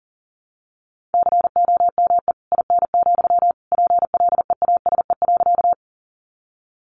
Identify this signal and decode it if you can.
Morse code, heard as a keyed tone, about 38 wpm, 700 Hz. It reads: COME IN, PLEASE.
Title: Sentences from Top 1500 Words 40wpm